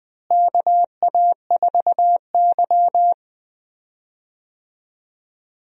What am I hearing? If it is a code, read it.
Morse code, heard as a keyed tone, about 20 wpm, 700 Hz. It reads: KA4Y
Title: USA Call Signs - 4 Characters - Repeated 2x - 20wpm